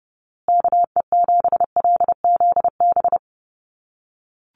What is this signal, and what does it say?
Morse code, heard as a keyed tone, about 30 wpm, 700 Hz. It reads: KE7LZB
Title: USA Call Signs Repeated 2x 30wpm